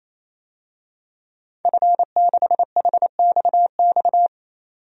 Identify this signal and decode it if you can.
Morse code, heard as a keyed tone, about 27 wpm, 700 Hz. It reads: F6HXX
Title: DXCC-100 Call Signs - 5 Characters - 28wpm